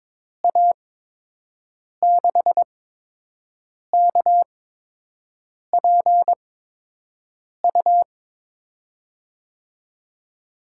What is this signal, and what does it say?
Morse code, heard as a keyed tone, about 22 wpm, spaced as for 6 wpm, 700 Hz. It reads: A6KPU
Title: USA Call Signs - 6 Characters - 8x Character Spacing - 22wpm